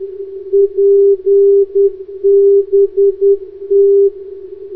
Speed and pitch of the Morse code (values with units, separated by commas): 10 wpm, 400 Hz